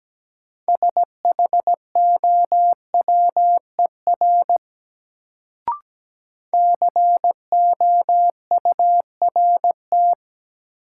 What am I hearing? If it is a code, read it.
Morse code, heard as a keyed tone, about 17 wpm, 700 Hz. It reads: SHOWER COURT
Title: Sentences from Top 1600 Words - New Word Drill Alt - 17wpm